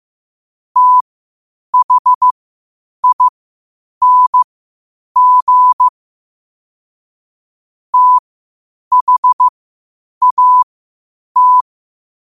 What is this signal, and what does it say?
Morse code, heard as a keyed tone, about 15 wpm, spaced as for 8 wpm, 1000 Hz. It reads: THING THAT
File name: qotdfiles / 8wpm